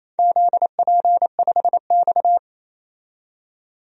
Morse code, heard as a keyed tone, about 28 wpm, 700 Hz. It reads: ZP5X
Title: DXCC-100 Call Signs - 4 Characters - 28wpm